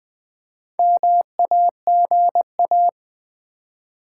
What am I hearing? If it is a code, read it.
Morse code, heard as a keyed tone, about 20 wpm, 700 Hz. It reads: MAGA
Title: Build A Word - 2 Letters At A Time - 8-Letter Words - 20wpm